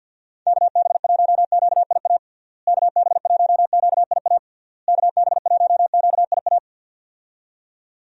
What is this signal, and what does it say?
Morse code, heard as a keyed tone, about 52 wpm, 700 Hz. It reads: KB1QIA KB1QIA KB1QIA
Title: USA Call Signs - 6 Characters - Repeated 3x - 50wpm